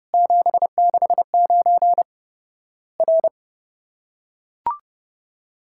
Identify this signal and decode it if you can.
Morse code, heard as a keyed tone, about 30 wpm, 700 Hz. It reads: 769 R
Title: Worked All Europe DX Contest - QTC Exchange - 30wpm